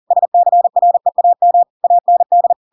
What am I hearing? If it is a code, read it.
Morse code, heard as a keyed tone, about 40 wpm, 700 Hz. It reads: SCREAM AND
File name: Pinocchio / Pinocchio ch7 40 40 ik1hge